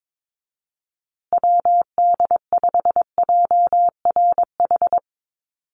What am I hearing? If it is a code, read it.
Morse code, heard as a keyed tone, about 22 wpm, 700 Hz. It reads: WD5JRH
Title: USA Call Signs Repeated 2x 22wpm